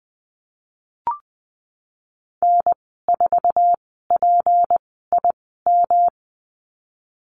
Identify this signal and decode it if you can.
Morse code, heard as a keyed tone, about 20 wpm, spaced as for 15 wpm, 700 Hz. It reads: N4PIM
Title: 161 Lesson 26g - Letter g - Call Signs 20wpm